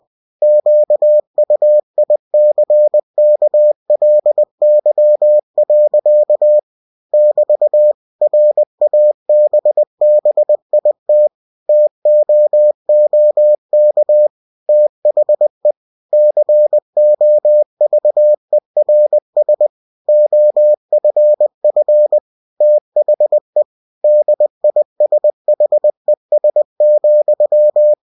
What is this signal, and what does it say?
Morse code, heard as a keyed tone, about 20 wpm, 600 Hz. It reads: QUICKLY. <BT> RABBIT TOOK THE COVERS OFF THE DISHES,
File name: cw-books / winnie-the-pooh / winnie-20wpm-page-0016